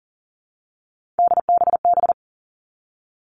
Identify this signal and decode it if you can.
Morse code, heard as a keyed tone, about 42 wpm, 700 Hz. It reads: DBB
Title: Three Challenging Letter-Number Combinations 40wpm